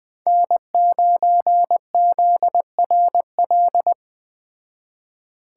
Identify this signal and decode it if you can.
Morse code, heard as a keyed tone, about 20 wpm, 700 Hz. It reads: N9ZRL